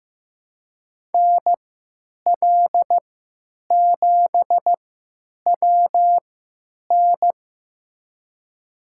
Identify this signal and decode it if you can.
Morse code, heard as a keyed tone, about 15 wpm, spaced as for 9 wpm, 700 Hz. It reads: NL7WN
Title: USA Call Signs - 3x Character Spacing - 15wpm